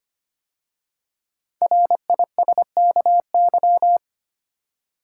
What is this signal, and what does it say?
Morse code, heard as a keyed tone, about 25 wpm, 700 Hz. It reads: RISKY
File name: 5 Letter Words - First 3 Letters Given - 25wpm